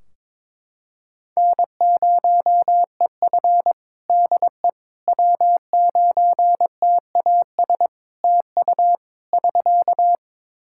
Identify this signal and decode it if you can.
Morse code, heard as a keyed tone, about 22 wpm, 700 Hz. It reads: N0EF DE W9TAS TU <SK>
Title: Practice QSO - Exchange 4 - Ending - 22wpm